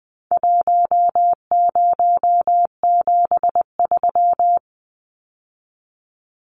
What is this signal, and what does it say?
Morse code, heard as a keyed tone, about 20 wpm, 700 Hz. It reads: 1073
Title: Numbers - 4-Digit 20wpm